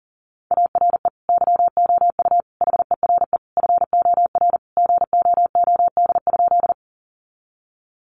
Morse code, heard as a keyed tone, about 40 wpm, 700 Hz. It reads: ARE YOU HERE FOR GOOD?
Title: Sentences from Top 200 Words 40wpm